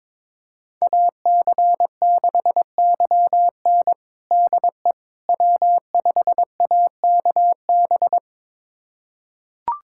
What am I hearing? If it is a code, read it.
Morse code, heard as a keyed tone, about 22 wpm, 700 Hz. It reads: AC6YN DE W5AKB
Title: USA Callsign DE Callsign - 5 Characters - 22wpm